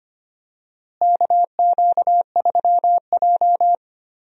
Morse code, heard as a keyed tone, about 25 wpm, 700 Hz. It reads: KQ3J
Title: USA Call Signs - 4 Characters - Repeated 3x - 25wpm